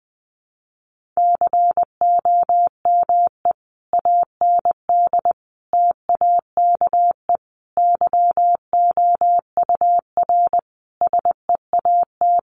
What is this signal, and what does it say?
Morse code, heard as a keyed tone, about 19 wpm, 700 Hz. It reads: COME AND TAKE YOUR SEAT
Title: Sentences from Top 1500 Words 20wpm